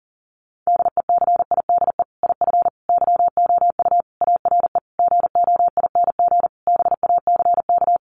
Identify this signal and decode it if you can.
Morse code, heard as a keyed tone, about 40 wpm, 700 Hz. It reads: DECIDE IF YOU ARE GOING BACK
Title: Sentences from Top 500 Words 40wpm